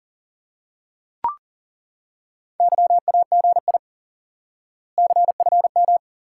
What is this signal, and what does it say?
Morse code, heard as a keyed tone, about 40 wpm, 700 Hz. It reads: YAGI CFM
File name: IQE Course 3 - Lesson 89 - YAGI - Two Elements - 40wpm